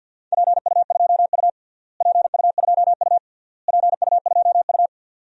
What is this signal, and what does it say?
Morse code, heard as a keyed tone, about 50 wpm, 700 Hz. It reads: PU2U PU2U PU2U
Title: DXCC-100 Call Signs - 4 Characters - Repeated 3x - 50wpm